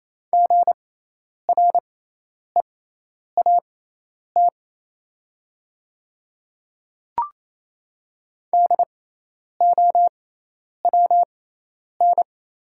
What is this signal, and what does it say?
Morse code, heard as a keyed tone, about 28 wpm, spaced as for 10 wpm, 690 Hz. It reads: GREAT DOWN
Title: 6x Character Spacing - Top 200 Words - 28wpm